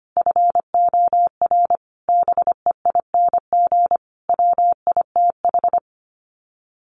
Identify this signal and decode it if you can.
Morse code, heard as a keyed tone, about 25 wpm, 700 Hz. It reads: FOR BEING WITH
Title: Sets of 5 Words 25wpm